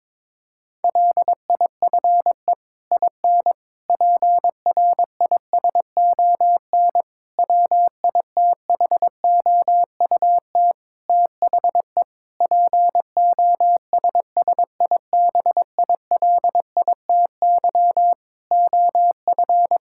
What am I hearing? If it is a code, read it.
Morse code, heard as a keyed tone, about 22 wpm, 700 Hz. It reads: LIFE IN PRISON WITHOUT THE POSSIBILITY OF
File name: practice / 22wpm / Sets of 8 Words - 22wpm